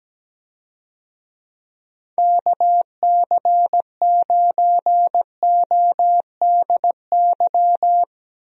Morse code, heard as a keyed tone, about 17 wpm, 700 Hz. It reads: KC9ODY
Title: USA Call Signs - 6 Characters - Repeated 3x - 17wpm